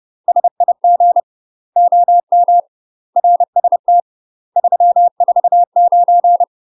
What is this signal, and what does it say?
Morse code, heard as a keyed tone, about 30 wpm, 700 Hz. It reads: SIG OM RST 349
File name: UK MASTER QSOs TRAINING / 30WPM / UK MASTER QSOs 0009